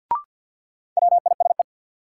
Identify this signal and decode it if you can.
Morse code, heard as a keyed tone, about 50 wpm, 700 Hz. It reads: WISE
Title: Sentences from Top 1600 Words - New Word Drill Alt - 50wpm